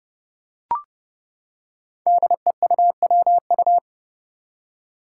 Morse code, heard as a keyed tone, about 30 wpm, 700 Hz. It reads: DEUWU
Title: Five Challenging Letter-Number Combinations 30wpm